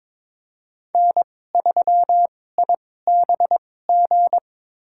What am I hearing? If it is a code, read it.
Morse code, heard as a keyed tone, about 22 wpm, spaced as for 16 wpm, 700 Hz. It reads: N3IBG
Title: USA Call Signs - 5 Characters - 2x Character Spacing - 22wpm